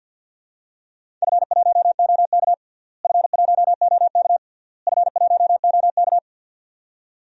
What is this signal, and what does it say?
Morse code, heard as a keyed tone, about 50 wpm, 700 Hz. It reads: F1OK F1OK F1OK
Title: DXCC-100 Call Signs - 4 Characters - Repeated 3x - 50wpm